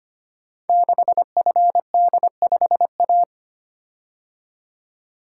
Morse code, heard as a keyed tone, about 25 wpm, 700 Hz. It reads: 6FD5A
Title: Code Groups - 5 Letter:Numbers - 25wpm